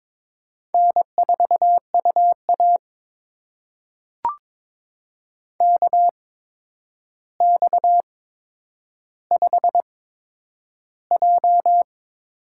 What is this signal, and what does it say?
Morse code, heard as a keyed tone, about 22 wpm, spaced as for 9 wpm, 700 Hz. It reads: N4UA KX5J
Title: USA Call Signs - 4 Characters - 8x Character Spacing - 22wpm